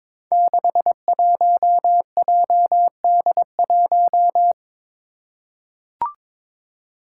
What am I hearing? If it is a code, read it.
Morse code, heard as a keyed tone, about 22 wpm, 700 Hz. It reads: 61JD1
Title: Five Challenging Letter-Number Combinations 22wpm